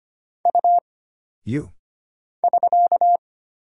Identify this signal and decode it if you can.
Morse code, heard as a keyed tone, about 25 wpm, 700 Hz. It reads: U <SK>
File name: Single Character (Letter:Number:Punctuation:Prosign) - Mind-Melt - 25wpm